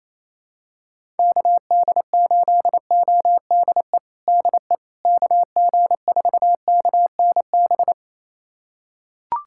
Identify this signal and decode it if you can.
Morse code, heard as a keyed tone, about 28 wpm, 700 Hz. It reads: KD8ODE DE KG4KNB
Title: USA Callsign DE Callsign - 6 Characters - 28wpm